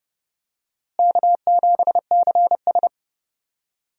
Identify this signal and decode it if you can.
Morse code, heard as a keyed tone, about 30 wpm, 700 Hz. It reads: K7CS
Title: USA Call Signs - 4 Characters - Repeated 2x - 30wpm